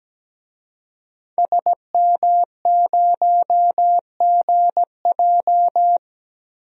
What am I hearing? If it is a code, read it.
Morse code, heard as a keyed tone, about 17 wpm, 700 Hz. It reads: SM0GJ
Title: DXCC-100 Call Signs - 5 Characters - Repeated 3x - 17wpm